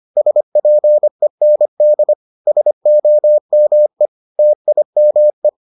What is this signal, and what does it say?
Morse code, heard as a keyed tone, about 25 wpm, 600 Hz. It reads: SPEND SOME TIME